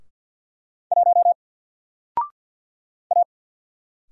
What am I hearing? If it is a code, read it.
Morse code, heard as a keyed tone, about 50 wpm, 700 Hz. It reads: 1 A
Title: Single Letter-Number 50wpm